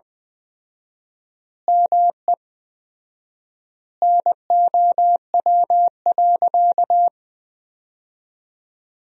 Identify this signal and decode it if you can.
Morse code, heard as a keyed tone, about 21 wpm, 700 Hz. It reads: ME NOW.
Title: 4x Word-Spacing - Sentences from Top 200 Words - 20wpm